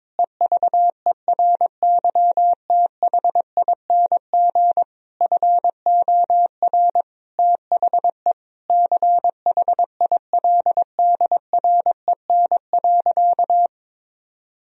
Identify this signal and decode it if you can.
Morse code, heard as a keyed tone, about 22 wpm, 700 Hz. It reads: EVERYTHING FOR THE CHILDREN.
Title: Sentences from Top 1000 Words - Review Part 2 - 22wpm